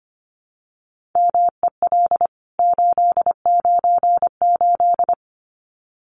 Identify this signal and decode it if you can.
Morse code, heard as a keyed tone, about 25 wpm, 700 Hz. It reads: MEL 898
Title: CWT Exchange 25wpm